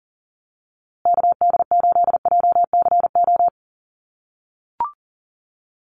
Morse code, heard as a keyed tone, about 37 wpm, 700 Hz. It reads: KD8JCO
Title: USA Call Signs Repeated 3x 40wpm